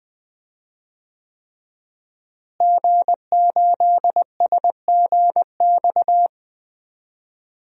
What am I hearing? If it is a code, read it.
Morse code, heard as a keyed tone, about 20 wpm, 700 Hz. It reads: G8SGX
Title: DXCC-100 Call Signs - 5 Characters - Repeated 3x - 20wpm